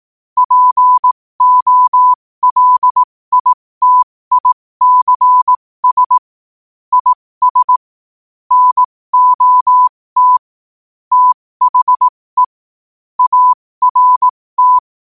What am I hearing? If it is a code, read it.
Morse code, heard as a keyed tone, about 18 wpm, spaced as for 15 wpm, 1000 Hz. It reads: POLITICS IS NOT THE ART
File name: qotdfiles / 15wpm